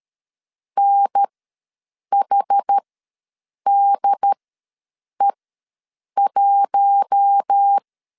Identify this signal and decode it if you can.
Morse code, heard as a keyed tone, about 13 wpm, spaced as for 7 wpm, 800 Hz. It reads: NHDE1
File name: LEZIONI / 7 GF9Z / 7 GF ws 12wpm 3140